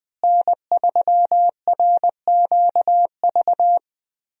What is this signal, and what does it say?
Morse code, heard as a keyed tone, about 20 wpm, 700 Hz. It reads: N3RQV
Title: USA Call Signs - 5 Characters - Repeated 3x - 20wpm